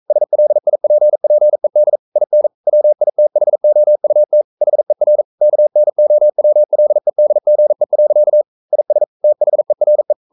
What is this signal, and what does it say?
Morse code, heard as a keyed tone, about 42 wpm, 600 Hz. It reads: SLIPPED IN WITHOUT HER KNOWLEDGE. IS THERE
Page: Romanen Never af Ken Follet på engelsk.